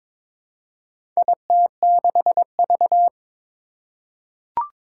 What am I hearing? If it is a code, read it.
Morse code, heard as a keyed tone, about 21 wpm, 700 Hz. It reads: IT6V
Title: Four Challenging Letter-Number Combinations 22wpm